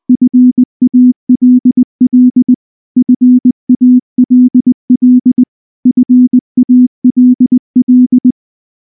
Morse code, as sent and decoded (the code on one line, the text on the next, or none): ..-. .- .-.. .-.. / ..-. .- .-.. .-.. / ..-. .- .-.. .-..
FALL FALL FALL